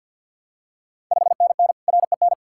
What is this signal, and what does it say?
Morse code, heard as a keyed tone, about 47 wpm, 700 Hz. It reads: 5NN REN
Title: Ontario QSO Party Exchange 50wpm